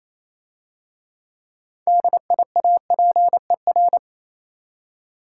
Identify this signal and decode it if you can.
Morse code, heard as a keyed tone, about 27 wpm, 700 Hz. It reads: DIAPER
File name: Easy Words - Speed-Racing with Hints - 22wpm